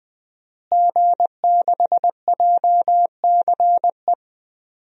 Morse code, heard as a keyed tone, about 21 wpm, 700 Hz. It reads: G6JCE
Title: DXCC-100 Call Signs - 5 Characters - Repeated 2x - 20wpm